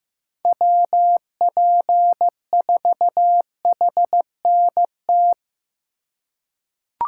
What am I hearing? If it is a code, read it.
Morse code, heard as a keyed tone, about 15 wpm, 700 Hz. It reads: WP4HNT
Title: USA Call Signs - 6 Characters - 4x Character Spacing - 15wpm